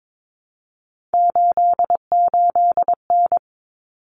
Morse code, heard as a keyed tone, about 22 wpm, 700 Hz. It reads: 88N
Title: Common Cut Numbers - 3-Digit 22wpm